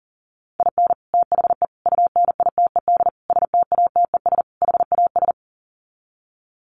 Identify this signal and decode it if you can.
Morse code, heard as a keyed tone, about 40 wpm, 700 Hz. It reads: IN THE UNITED STATES HAS
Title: Sets of 5 Words Encore2 40wpm